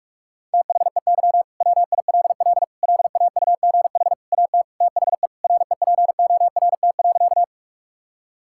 Morse code, heard as a keyed tone, about 44 wpm, 690 Hz. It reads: THEY WILL LAUGH AT THE REPORT.
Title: Sentences from Top 1000 Words - Review Part 6 - 45wpm